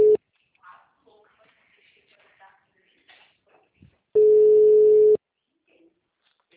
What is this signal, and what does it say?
Ringback tone (Europe, 425 Hz), with speech.